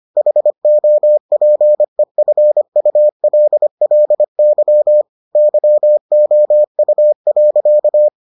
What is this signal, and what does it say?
Morse code, heard as a keyed tone, about 25 wpm, 600 Hz. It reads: HOPEFULLY YOU.